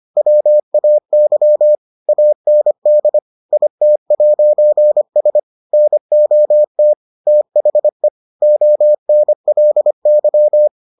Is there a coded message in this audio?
Yes: Morse code from 0.2 to 10.7 s, heard as a keyed tone, about 25 wpm, 600 Hz. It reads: WAY AND IT'S NOT THE ONLY